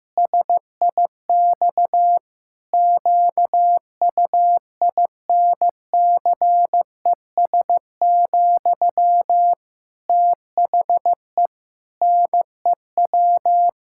Morse code, heard as a keyed tone, about 15 wpm, 700 Hz. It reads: SIX QUINCES, THE NEW
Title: Pangram Challenge v2 - 15wpm